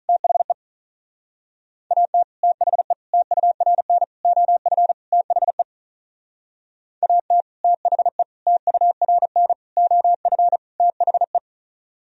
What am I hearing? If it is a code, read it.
Morse code, heard as a keyed tone, about 45 wpm, 700 Hz. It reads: THE AT THE TURN OF THE AT THE TURN OF THE